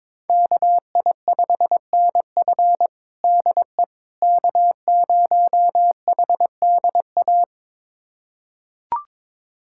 Morse code, heard as a keyed tone, about 22 wpm, 700 Hz. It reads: KI5NF DE K0HDA
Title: USA Callsign DE Callsign - 5 Characters - 22wpm